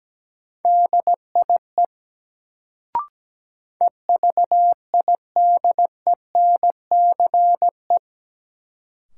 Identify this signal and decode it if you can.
Morse code, heard as a keyed tone, about 17 wpm, 700 Hz. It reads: DIE EVIDENCE
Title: Sentences from Top 1300 Words - New Word Drill Alt - 17wpm